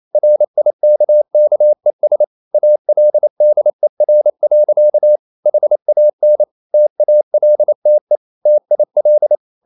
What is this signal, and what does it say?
Morse code, heard as a keyed tone, about 28 wpm, 600 Hz. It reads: RIKKES ALDER. HAN TALTE TIL
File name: Ferie QRQ 28wpm